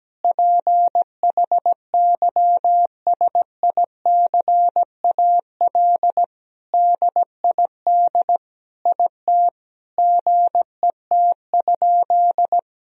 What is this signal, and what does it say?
Morse code, heard as a keyed tone, about 17 wpm, 700 Hz. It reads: PHYSICAL DID IT GET?
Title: Sentences from Top 1000 Words - Review Part 4 - 17wpm